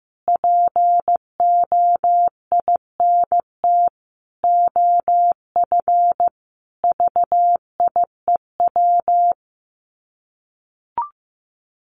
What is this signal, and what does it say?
Morse code, heard as a keyed tone, about 14 wpm, 700 Hz. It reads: POINT OF VIEW
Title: Sets of 5 Words 15wpm